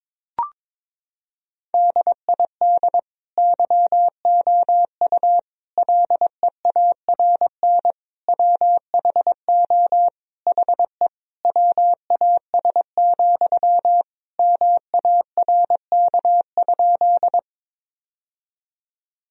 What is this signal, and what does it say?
Morse code, heard as a keyed tone, about 22 wpm, 700 Hz. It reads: DID YOU LEARN WHO HE WAS, MARK?
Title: Sentences from Top 1000 Words - Review Part 1 - 22wpm